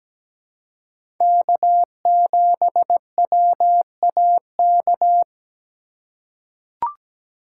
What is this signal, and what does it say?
Morse code, heard as a keyed tone, about 17 wpm, 700 Hz. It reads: K7WAK
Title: USA Call Signs - 5 Characters - 4x Character Spacing - 17wpm